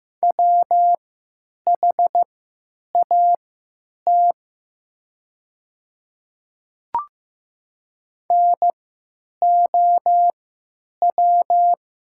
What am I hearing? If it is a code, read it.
Morse code, heard as a keyed tone, about 15 wpm, spaced as for 9 wpm, 700 Hz. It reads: WHAT NOW
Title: 3x Character Spacing - Top 100 Words - 15wpm